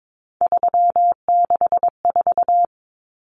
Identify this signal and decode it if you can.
Morse code, heard as a keyed tone, about 22 wpm, 700 Hz. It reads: 364
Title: Numbers - 3-Digit 22wpm